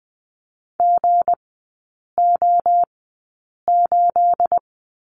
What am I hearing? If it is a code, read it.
Morse code, heard as a keyed tone, about 19 wpm, 700 Hz. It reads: G O 8
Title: 212 Lesson 34e - Number 8 - Three Characters 20wpm